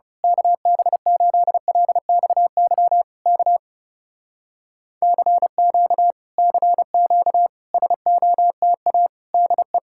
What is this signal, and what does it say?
Morse code, heard as a keyed tone, about 28 wpm, 700 Hz. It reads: KB8LXY K CQ CQ SOTA DE
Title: Speed-Racing Common QSO Phrases - 30wpm